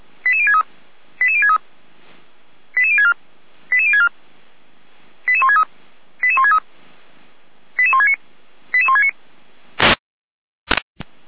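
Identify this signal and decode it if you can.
Five-tone selective call (ZVEI), heard as quick runs of five tones, tones of about 70 ms. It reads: 88063 88063 88064 88064 88163 88163 88169 88169
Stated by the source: ZVEI five-tone call 88521 received by SDR.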